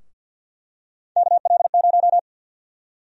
Morse code, heard as a keyed tone, about 52 wpm, 700 Hz. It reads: KL0
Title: USA Call Sign Prefixes - 50wpm